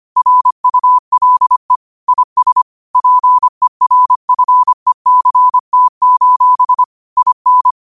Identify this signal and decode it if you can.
Morse code, heard as a keyed tone, about 25 wpm, 1000 Hz. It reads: RULE IS PERFECT: IN
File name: qotdfiles / 25wpm